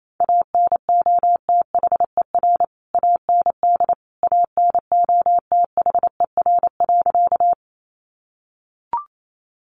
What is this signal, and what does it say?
Morse code, heard as a keyed tone, about 28 wpm, 700 Hz. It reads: ANOTHER AND ANOTHER.
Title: Sentences from Top 600 Words 28wpm